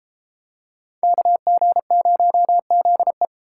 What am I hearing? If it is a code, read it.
Morse code, heard as a keyed tone, about 31 wpm, 700 Hz. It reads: KG0ZE